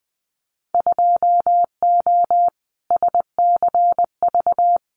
Morse code, heard as a keyed tone, about 20 wpm, 700 Hz. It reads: 2O SCV
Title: Winter Field Day Exchanges Encore 20wpm